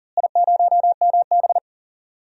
Morse code, heard as a keyed tone, about 40 wpm, 700 Hz. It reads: I0MB
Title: DXCC-100 Call Signs - 4 Characters - Repeated 3x - 40wpm